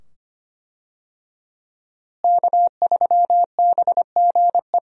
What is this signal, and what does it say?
Morse code, heard as a keyed tone, about 25 wpm, 700 Hz. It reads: K3BGE
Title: USA Call Signs - 5 Characters - Repeated 2x - 25wpm